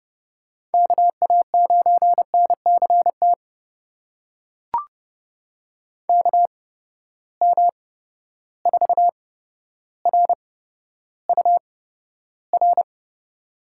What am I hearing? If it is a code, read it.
Morse code, heard as a keyed tone, about 30 wpm, spaced as for 13 wpm, 700 Hz. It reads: KA9NCT KM4RUR
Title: USA Call Signs - 6 Characters - 8x Character Spacing - 30wpm